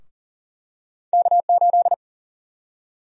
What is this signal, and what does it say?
Morse code, heard as a keyed tone, about 40 wpm, 700 Hz. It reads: K8
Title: USA Call Sign Prefixes - Repeated 2x - 40wpm